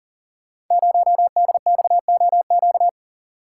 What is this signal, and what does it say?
Morse code, heard as a keyed tone, about 40 wpm, 700 Hz. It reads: 0DXOQ
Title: Code Groups - 5 Letter:Numbers - 40wpm